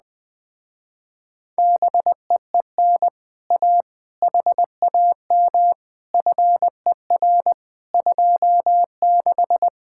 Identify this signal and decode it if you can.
Morse code, heard as a keyed tone, about 20 wpm, 700 Hz. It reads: BEEN A HAM FER 26
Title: Speed-Racing Common QSO Phrases Encore - 17wpm